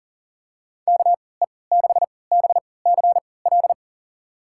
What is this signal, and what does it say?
Morse code, heard as a keyed tone, about 40 wpm, spaced as for 23 wpm, 700 Hz. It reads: KE6BCL
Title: USA Call Signs - 6 Characters - 3x Character Spacing - 40wpm